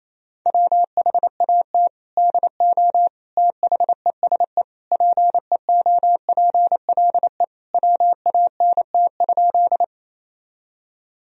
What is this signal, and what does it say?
Morse code, heard as a keyed tone, about 28 wpm, 700 Hz. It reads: WHAT DO THESE PEOPLE WANT?
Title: Sentences from Top 1000 Words - Review Part 1 - 28wpm